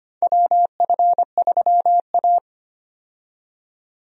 Morse code, heard as a keyed tone, about 25 wpm, 700 Hz. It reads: WF3A
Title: USA Call Signs - 4 Characters - 25wpm